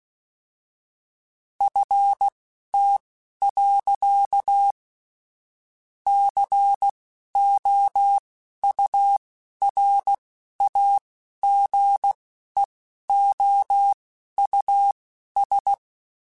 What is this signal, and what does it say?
Morse code, heard as a keyed tone, about 16 wpm, spaced as for 11 wpm, 800 Hz. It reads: FT. COURAGEOUS